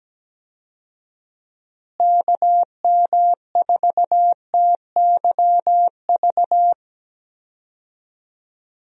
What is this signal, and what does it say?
Morse code, heard as a keyed tone, about 17 wpm, 700 Hz. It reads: KM4TYV
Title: USA Call Signs - 6 Characters - Repeated 3x - 17wpm